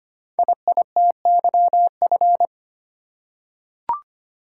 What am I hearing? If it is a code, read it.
Morse code, heard as a keyed tone, about 25 wpm, 700 Hz. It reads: IITYF
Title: Code Groups - 5 Letters - 25wpm